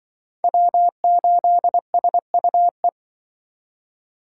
Morse code, heard as a keyed tone, about 25 wpm, 700 Hz. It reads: W8SUE
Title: Speed Racing - USA Call Signs 20wpm